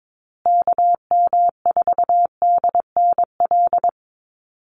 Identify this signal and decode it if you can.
Morse code, heard as a keyed tone, about 22 wpm, 700 Hz. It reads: KM4DNL
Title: USA Call Signs Repeated 3x 22wpm